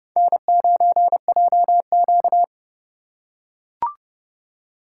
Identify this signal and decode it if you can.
Morse code, heard as a keyed tone, about 29 wpm, 700 Hz. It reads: N9JQ